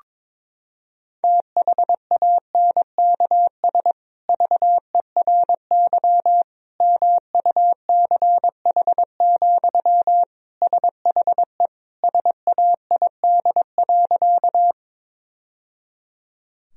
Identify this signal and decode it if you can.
Morse code, heard as a keyed tone, about 22 wpm, 700 Hz. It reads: THANKS VERY MUCH, SHE SAID.
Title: Sentences from Top 1000 Words - Review Part 3 - 22wpm